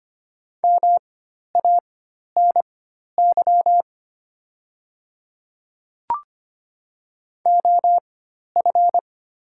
Morse code, heard as a keyed tone, about 25 wpm, spaced as for 12 wpm, 700 Hz. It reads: MANY OF
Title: 4x Character Spacing - Top 100 Words - 25wpm